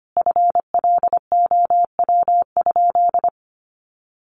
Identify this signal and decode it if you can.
Morse code, heard as a keyed tone, about 25 wpm, 700 Hz. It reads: FLOW?
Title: Sentences from Top 1200 Words 25wpm